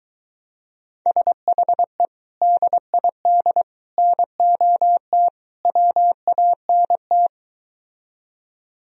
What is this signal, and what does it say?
Morse code, heard as a keyed tone, about 23 wpm, 700 Hz. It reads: SHE DID NOT WANT